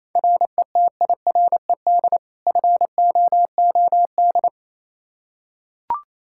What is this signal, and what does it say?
Morse code, heard as a keyed tone, about 28 wpm, 700 Hz. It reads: RETIRED FOOD
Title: Practice QSO - Exchange 3 - More Info - 28wpm